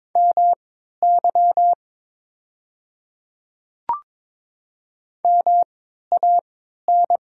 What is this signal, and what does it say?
Morse code, heard as a keyed tone, about 22 wpm, spaced as for 12 wpm, 700 Hz. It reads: MY MAN
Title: 3x Character Spacing - Top 500 Words - 22wpm